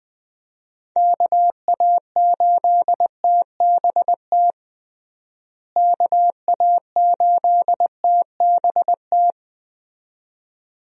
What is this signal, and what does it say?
Morse code, heard as a keyed tone, about 20 wpm, 700 Hz. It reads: KA8TBT KA8TBT
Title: USA Call Signs - 6 Characters - Repeated 2x - 20wpm